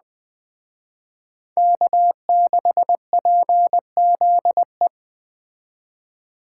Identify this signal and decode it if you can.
Morse code, heard as a keyed tone, about 21 wpm, 700 Hz. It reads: K6PZE
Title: Speed Racing - USA 1x3 Call Signs 17wpm